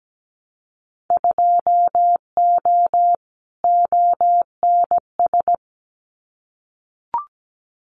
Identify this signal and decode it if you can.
Morse code, heard as a keyed tone, about 17 wpm, 700 Hz. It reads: 2O ONS
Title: Winter Field Day Exchanges Encore 17wpm